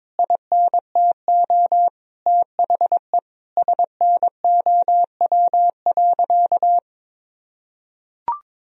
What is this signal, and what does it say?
Morse code, heard as a keyed tone, about 22 wpm, 700 Hz. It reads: INTO THE SNOW.
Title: Sentences from Top 1000 Words - Review Part 3 - 22wpm